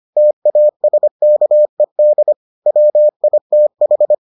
Morse code, heard as a keyed tone, about 25 wpm, 600 Hz. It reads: TASKED WITH